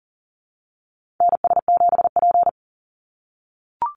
Morse code, heard as a keyed tone, about 40 wpm, 700 Hz. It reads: NS7P
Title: NA SOTA Activator Callsigns 40wpm